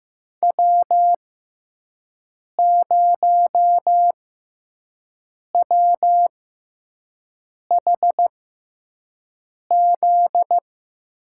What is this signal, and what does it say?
Morse code, heard as a keyed tone, about 15 wpm, spaced as for 5 wpm, 700 Hz. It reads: W0WHZ
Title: USA Call Signs - 5 Characters - 6x Character Spacing - 15wpm